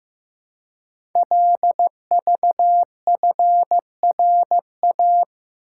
Morse code, heard as a keyed tone, about 15 wpm, 700 Hz. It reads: LVFRA
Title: Code Groups - 5 Letters - 15wpm